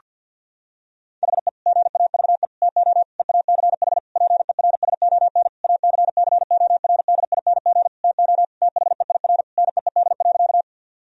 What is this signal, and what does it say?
Morse code, heard as a keyed tone, about 50 wpm, 700 Hz. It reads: HE GAVE TO EACH PERSON ACCORDING TO THEIR NEED.
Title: Sentences from Top 1000 Words - Review Part 4 - 50wpm